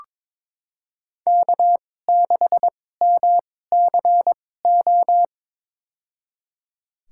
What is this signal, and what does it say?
Morse code, heard as a keyed tone, about 22 wpm, spaced as for 16 wpm, 700 Hz. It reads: K6MCO
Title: USA Call Signs - 2x Character Spacing - 22wpm